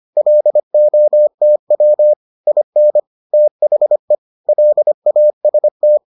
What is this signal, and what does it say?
Morse code, heard as a keyed tone, about 25 wpm, 600 Hz. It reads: LOTW IN THE LAST